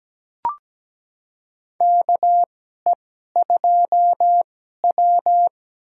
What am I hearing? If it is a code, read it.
Morse code, heard as a keyed tone, about 17 wpm, spaced as for 12 wpm, 700 Hz. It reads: KE2W
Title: USA Call Signs - 4 Characters - 2x Character Spacing - 17wpm